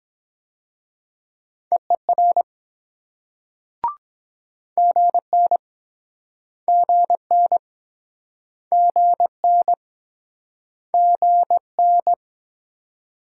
Morse code, heard as a keyed tone, about 22 wpm, 700 Hz. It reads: EER GN GN GN GN